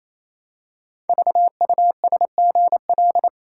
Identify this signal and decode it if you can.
Morse code, heard as a keyed tone, about 28 wpm, 700 Hz. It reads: VUSGL
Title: Code Groups - 5 Letters - 28wpm